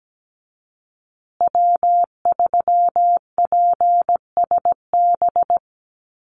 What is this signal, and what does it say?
Morse code, heard as a keyed tone, about 17 wpm, 700 Hz. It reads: W3PSB
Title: USA Call Signs Repeated 2x 17wpm